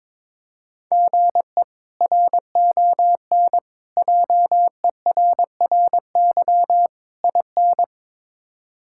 Morse code, heard as a keyed tone, about 22 wpm, 700 Hz. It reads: GE RON JERRY IN
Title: K1USN SST QSO Practice - 22wpm